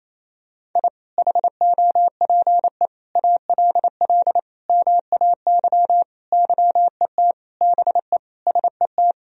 Morse code, heard as a keyed tone, about 27 wpm, 700 Hz. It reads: I HOPE ALL MAY YET BE SET
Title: Sentences from Top 1000 Words - Review Part 1 - 28wpm